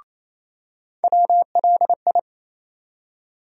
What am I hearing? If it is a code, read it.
Morse code, heard as a keyed tone, about 27 wpm, 700 Hz. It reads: WLI
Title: USA Call Sign Postfixes - 3 Letters 28wpm